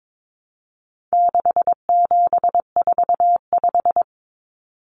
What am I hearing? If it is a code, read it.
Morse code, heard as a keyed tone, about 22 wpm, 700 Hz. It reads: 6745
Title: Numbers - 4-Digit 22wpm